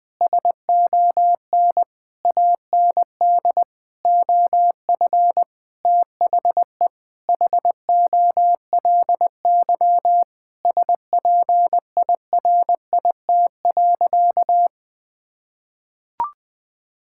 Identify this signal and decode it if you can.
Morse code, heard as a keyed tone, about 20 wpm, 700 Hz. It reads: SON AND OF THE HOLY SPIRIT.
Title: Sentences with 9-Word Phrases - 20wpm